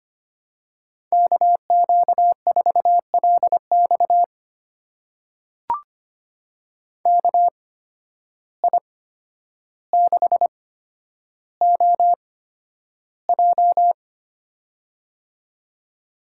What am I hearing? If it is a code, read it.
Morse code, heard as a keyed tone, about 25 wpm, spaced as for 11 wpm, 700 Hz. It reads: KQ4LX KI6OJ
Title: USA Call Signs - 5 Characters - 8x Character Spacing - 25wpm